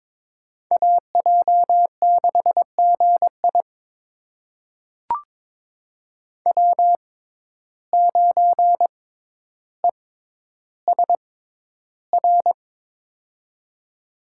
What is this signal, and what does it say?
Morse code, heard as a keyed tone, about 22 wpm, spaced as for 11 wpm, 700 Hz. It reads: AJ6GI W9ESR
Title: USA Call Signs - 5 Characters - 6x Character Spacing - 22wpm